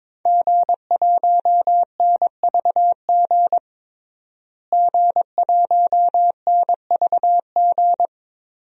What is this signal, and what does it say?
Morse code, heard as a keyed tone, about 22 wpm, 700 Hz. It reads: G1NVG G1NVG
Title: DXCC-100 Call Signs - 5 Characters - Repeated 2x - 22wpm